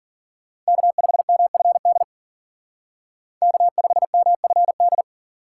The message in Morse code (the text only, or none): K5MFD K5MFD